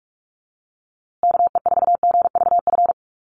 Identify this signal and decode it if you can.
Morse code, heard as a keyed tone, about 44 wpm, 700 Hz. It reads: KE4GVF
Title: USA Call Signs Repeated 3x 45wpm